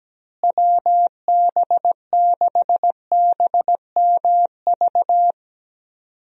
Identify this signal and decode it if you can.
Morse code, heard as a keyed tone, about 17 wpm, 700 Hz. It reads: WB6BMV
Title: USA Call Signs - 6 Characters - Repeated 2x - 17wpm